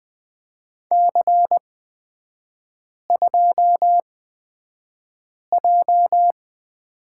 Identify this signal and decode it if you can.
Morse code, heard as a keyed tone, about 20 wpm, spaced as for 5 wpm, 700 Hz. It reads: C2J
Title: 224 Lesson 36e - BT - Three Characters 20wpm